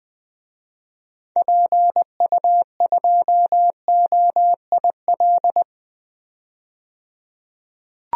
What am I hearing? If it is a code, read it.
Morse code, heard as a keyed tone, about 21 wpm, 700 Hz. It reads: PU2OIL
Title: DXCC-100 Call Signs - 6 Characters - 20wpm